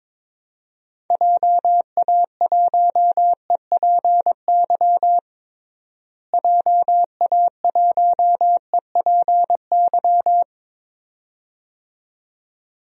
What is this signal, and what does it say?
Morse code, heard as a keyed tone, about 22 wpm, 700 Hz. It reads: JA1EPY JA1EPY
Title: DXCC-100 Call Signs - 6 Characters - Repeated 2x - 22wpm